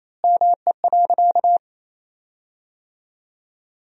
Morse code, heard as a keyed tone, about 28 wpm, 700 Hz. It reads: ME.
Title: 3x Word-Spacing - Sentences from Top 300 Words - 28wpm